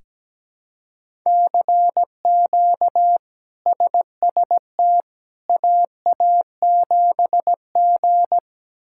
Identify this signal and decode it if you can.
Morse code, heard as a keyed tone, about 17 wpm, 700 Hz. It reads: CQ SST AA7G
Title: K1USN SST QSO Practice - 17wpm